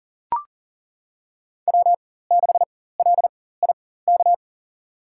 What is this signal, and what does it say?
Morse code, heard as a keyed tone, about 39 wpm, spaced as for 18 wpm, 700 Hz. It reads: W6LIK
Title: USA Call Signs - 5 Characters - 4x Character Spacing - 40wpm